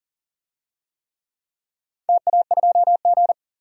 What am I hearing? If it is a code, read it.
Morse code, heard as a keyed tone, about 40 wpm, 700 Hz. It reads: TA2G
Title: DXCC-100 Call Signs - 4 Characters - 40wpm